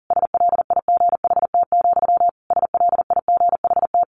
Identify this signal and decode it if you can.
Morse code, heard as a keyed tone, about 40 wpm, 700 Hz. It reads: SLIGHT, SLIGHT
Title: A Sound Of Thunder 40wpm